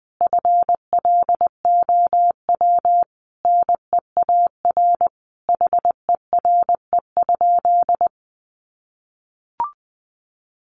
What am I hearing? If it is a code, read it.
Morse code, heard as a keyed tone, about 20 wpm, 700 Hz. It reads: FLOW NEAR HERE?
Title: Sentences from Top 1200 Words 20wpm